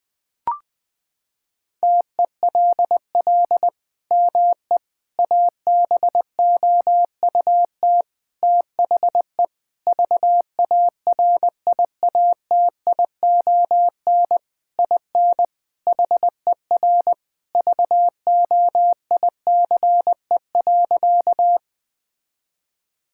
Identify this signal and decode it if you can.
Morse code, heard as a keyed tone, about 20 wpm, 700 Hz. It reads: TELL ME ABOUT THE VARIATION IN HER VOICE.
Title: Sentences from Top 1600 Words 20wpm